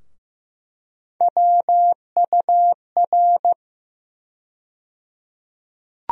Morse code, heard as a keyed tone, about 15 wpm, 700 Hz. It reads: WUR
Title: USA Call Sign Postfixes - 3 Letters 15wpm